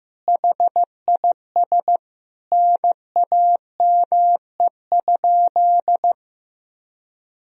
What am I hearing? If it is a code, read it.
Morse code, heard as a keyed tone, about 15 wpm, 700 Hz. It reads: HIS NAME?
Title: Sentences from Top 200 Words and Abbreviations 15wpm